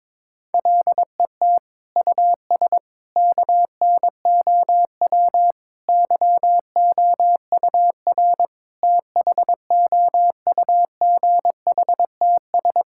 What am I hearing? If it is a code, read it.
Morse code, heard as a keyed tone, about 22 wpm, 700 Hz. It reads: LET US KNOW YOUR THOUGHTS
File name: practice / 22wpm / Sets of 8 Words - 22wpm